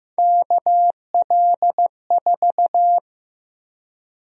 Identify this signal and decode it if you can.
Morse code, heard as a keyed tone, about 15 wpm, 700 Hz. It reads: KL4
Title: USA Call Sign Prefixes - Repeated 2x - 15wpm